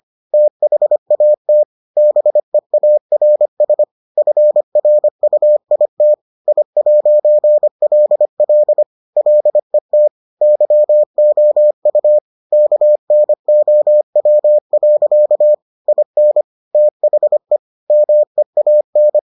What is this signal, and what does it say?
Morse code, heard as a keyed tone, about 25 wpm, 600 Hz. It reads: THAT BEARS FRUIT I'LL LET YOU KNOW. IN THE MEAN